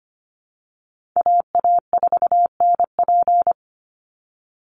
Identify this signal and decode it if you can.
Morse code, heard as a keyed tone, about 25 wpm, 700 Hz. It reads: AA4NP
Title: Call Signs Repeated 2x - 25wpm